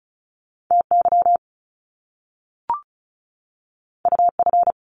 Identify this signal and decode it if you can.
Morse code, heard as a keyed tone, about 36 wpm, 700 Hz. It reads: TY UF
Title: Top Two Letter Combinations in Words 35wpm